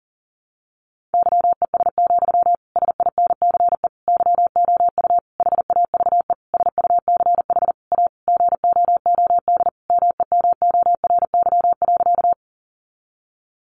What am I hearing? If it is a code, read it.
Morse code, heard as a keyed tone, about 40 wpm, 700 Hz. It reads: YES, SINCE YOU HAVE SUCH A GOOD MEMORY.
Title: Sentences from Top 1300 Words 40wpm